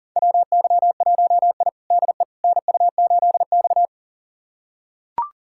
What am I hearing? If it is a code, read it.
Morse code, heard as a keyed tone, about 37 wpm, 700 Hz. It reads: WY1I DE NU8X
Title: USA Callsign DE Callsign - 4 Characters - 40wpm